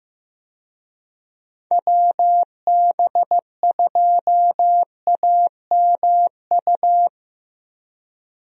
Morse code, heard as a keyed tone, about 15 wpm, 700 Hz. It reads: WB2AMU
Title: USA Call Signs - 6 Characters - 15wpm